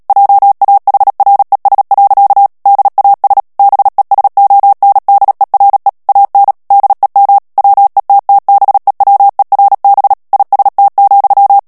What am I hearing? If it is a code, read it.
Morse code, heard as a keyed tone, about 37 wpm, 800 Hz. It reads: JAHRES. DAS BESONDERE AN DEM WETTBEWERB IST,
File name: morse mp3 / ALLEMAND texte clair 2 vitesse 35 mots minute